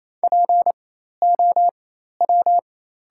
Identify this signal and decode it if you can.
Morse code, heard as a keyed tone, about 28 wpm, spaced as for 13 wpm, 690 Hz. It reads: POW
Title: 4x Character Spacing - Top 500 Words - 28wpm